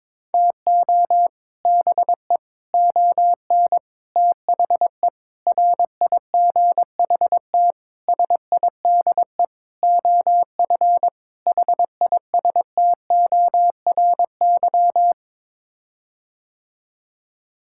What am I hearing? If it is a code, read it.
Morse code, heard as a keyed tone, about 22 wpm, 700 Hz. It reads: TO BE ON THE RIGHT SIDE OF HISTORY
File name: practice / 22wpm / Sets of 8 Words - 22wpm